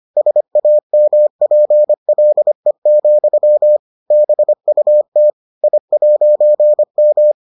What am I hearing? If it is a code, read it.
Morse code, heard as a keyed tone, about 25 wpm, 600 Hz. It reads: SAMPLE, BUT I'M